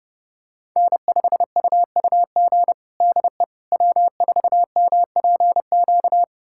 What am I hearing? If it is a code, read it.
Morse code, heard as a keyed tone, about 30 wpm, 700 Hz. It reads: N5UUG DE W4MPQ
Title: USA Callsign DE Callsign - 5 Characters - 30wpm